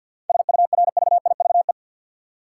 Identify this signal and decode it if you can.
Morse code, heard as a keyed tone, about 53 wpm, 700 Hz. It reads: SURVIVE
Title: Sentences from Top 1300 Words - New Word Drill Alt - 50wpm